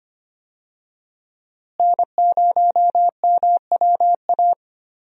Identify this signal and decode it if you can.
Morse code, heard as a keyed tone, about 25 wpm, 700 Hz. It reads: N0MWA
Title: USA Call Signs - 5 Characters - Repeated 2x - 25wpm